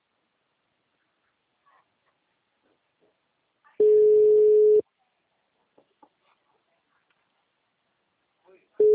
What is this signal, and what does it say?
Ringback tone (Europe, 425 Hz), with speech.